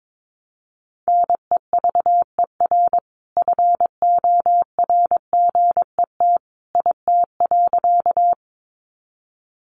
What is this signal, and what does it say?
Morse code, heard as a keyed tone, about 22 wpm, 700 Hz. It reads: NEVER FORGET IT.
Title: Sentences from Top 1200 Words 22wpm